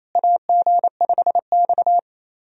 Morse code, heard as a keyed tone, about 28 wpm, 700 Hz. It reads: AG5X
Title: USA Call Signs - 4 Characters - Repeated 3x - 28wpm